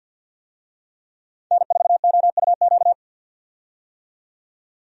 Morse code, heard as a keyed tone, about 47 wpm, 700 Hz. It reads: N4OUQ
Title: USA Call Signs - 5 Characters - 50wpm